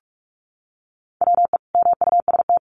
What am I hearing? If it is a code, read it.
Morse code, heard as a keyed tone, about 44 wpm, 700 Hz. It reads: WE MUST
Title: Sentences from Top 1400 Words 45wpm